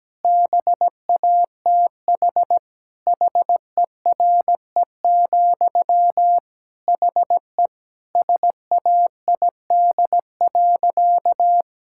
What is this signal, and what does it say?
Morse code, heard as a keyed tone, about 17 wpm, 700 Hz. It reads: BATH HERE, HE SAID.
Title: Sentences from Top 1600 Words 17wpm